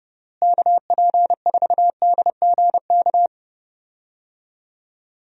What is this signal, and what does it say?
Morse code, heard as a keyed tone, about 32 wpm, 700 Hz. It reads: KP4DGK
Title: USA Call Signs - 6 Characters - Repeated 2x - 30wpm